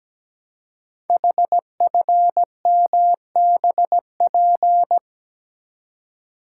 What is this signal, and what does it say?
Morse code, heard as a keyed tone, about 17 wpm, 700 Hz. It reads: HFMBP
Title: Code Groups - 5 Letter:Numbers - 17wpm